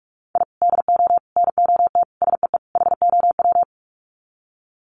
Morse code, heard as a keyed tone, about 44 wpm, 700 Hz. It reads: I DO NOT SEE HOW
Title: Sets of 5 Words 45wpm